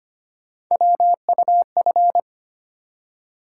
Morse code, heard as a keyed tone, about 25 wpm, 700 Hz. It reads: WUF
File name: Three Letter-Number Combinations - 25wpm